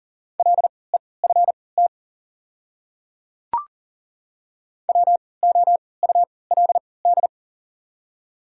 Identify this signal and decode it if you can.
Morse code, heard as a keyed tone, about 37 wpm, spaced as for 22 wpm, 700 Hz. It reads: LEFT WOULD
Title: 3x Character Spacing - Top 300 Words - 40wpm